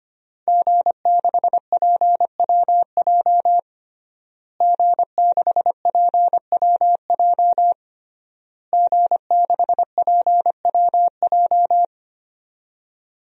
Morse code, heard as a keyed tone, about 25 wpm, 700 Hz. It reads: G6PWJ G6PWJ G6PWJ
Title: DXCC-100 Call Signs - 5 Characters - Repeated 3x - 25wpm